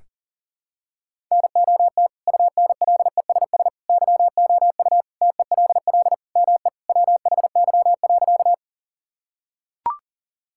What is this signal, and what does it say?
Morse code, heard as a keyed tone, about 40 wpm, 700 Hz. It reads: NOT UNLESS YOU TELL ME WHY.
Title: Sentences from Top 1000 Words - Review Part 6 - 40wpm